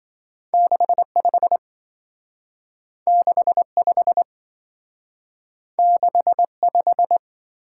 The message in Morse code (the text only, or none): 65 65 65